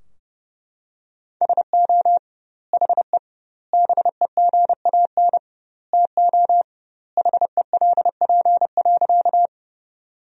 Morse code, heard as a keyed tone, about 30 wpm, 700 Hz. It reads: SO HE BEGAN TO HELP.
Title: 2x Word-Spacing - Sentences from Top 300 Words - 30wpm